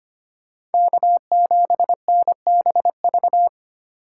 Morse code, heard as a keyed tone, about 32 wpm, 700 Hz. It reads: K7NBV